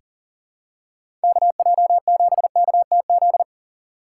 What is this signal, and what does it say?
Morse code, heard as a keyed tone, about 40 wpm, 700 Hz. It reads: KJ7KTZ
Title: USA Call Signs - 6 Characters - 2x Character Spacing - 40wpm